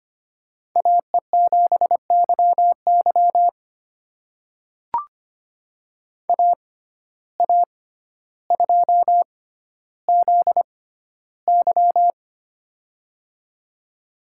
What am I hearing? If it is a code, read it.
Morse code, heard as a keyed tone, about 25 wpm, spaced as for 13 wpm, 700 Hz. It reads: AE7YY AA2ZY
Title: USA Call Signs - 5 Characters - 6x Character Spacing - 25wpm